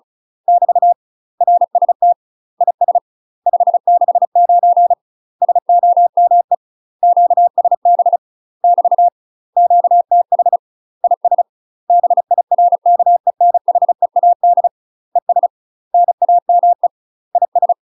Morse code, heard as a keyed tone, about 36 wpm, 700 Hz. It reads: <BT> RST IS 569 SOME QSB <BT> QTH IS BIRKENHEAD ES NAME IS
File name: UK MASTER QSOs TRAINING / 35WPM / UK MASTER QSOs 0025